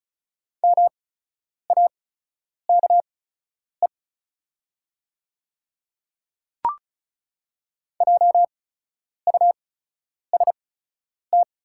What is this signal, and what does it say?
Morse code, heard as a keyed tone, about 34 wpm, spaced as for 10 wpm, 690 Hz. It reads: MAKE JUST
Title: 8x Character Spacing - Top 200 Words - 35wpm